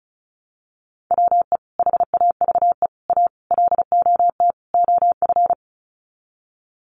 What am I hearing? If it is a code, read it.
Morse code, heard as a keyed tone, about 34 wpm, 700 Hz. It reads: WE HAVE A LOT OF
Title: Sets of 5 Words 35wpm